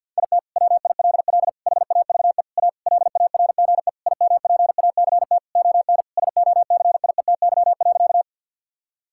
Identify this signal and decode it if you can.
Morse code, heard as a keyed tone, about 52 wpm, 700 Hz. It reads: IT WILL HAVE A LARGE IMPACT ON SOCIETY.
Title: Sentences from Top 1000 Words - Review Part 6 - 50wpm